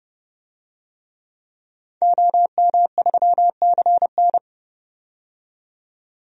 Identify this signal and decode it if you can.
Morse code, heard as a keyed tone, about 31 wpm, 700 Hz. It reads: OM3CN
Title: DXCC-100 Call Signs - 5 Characters - 30wpm